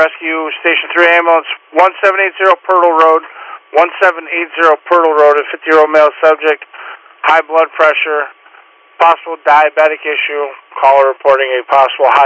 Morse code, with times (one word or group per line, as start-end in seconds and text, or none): none